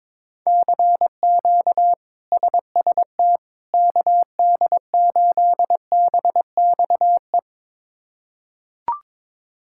Morse code, heard as a keyed tone, about 22 wpm, 700 Hz. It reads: CQ SST KD8BXE
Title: K1USN SST QSO Practice - 22wpm